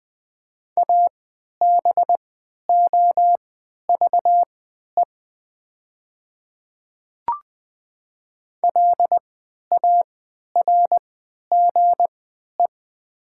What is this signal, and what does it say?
Morse code, heard as a keyed tone, about 20 wpm, spaced as for 11 wpm, 700 Hz. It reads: ABOVE LARGE
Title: 3x Character Spacing - Top 500 Words - 20wpm